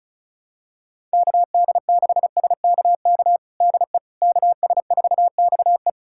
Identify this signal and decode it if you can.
Morse code, heard as a keyed tone, about 36 wpm, 700 Hz. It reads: KD6SKK DE KS4XE
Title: USA Callsign DE Callsign 35wpm